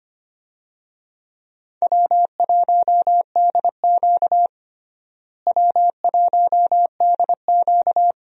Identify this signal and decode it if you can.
Morse code, heard as a keyed tone, about 25 wpm, 700 Hz. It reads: W1DQ W1DQ
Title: USA Call Signs - 4 Characters - Repeated 2x - 25wpm